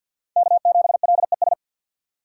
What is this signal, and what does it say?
Morse code, heard as a keyed tone, about 52 wpm, 700 Hz. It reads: K7LES
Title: USA Call Signs - 5 Characters - 2x Character Spacing - 50wpm